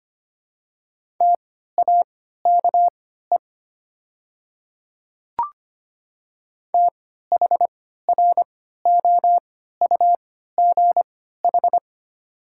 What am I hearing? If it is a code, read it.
Morse code, heard as a keyed tone, about 25 wpm, spaced as for 14 wpm, 700 Hz. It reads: TAKE THROUGH
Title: 3x Character Spacing - Top 200 Words - 25wpm